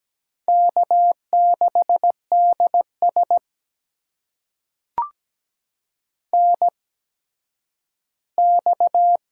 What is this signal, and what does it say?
Morse code, heard as a keyed tone, about 17 wpm, spaced as for 10 wpm, 700 Hz. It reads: K6DS NX
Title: USA Call Signs - 4 Characters - 8x Character Spacing - 17wpm